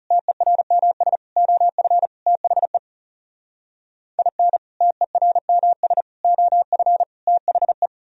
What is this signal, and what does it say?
Morse code, heard as a keyed tone, about 37 wpm, 700 Hz. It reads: TERMS OF THE IN TERMS OF THE
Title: Speed-Racing - Sets of 4 Words 30wpm